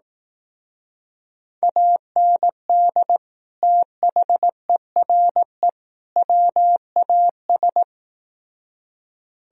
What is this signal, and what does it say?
Morse code, heard as a keyed tone, about 18 wpm, 700 Hz. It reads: AND THERE WAS
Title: Speed-Racing - Sets of 3 Words 15wpm